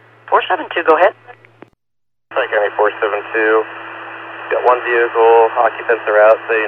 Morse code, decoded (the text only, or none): none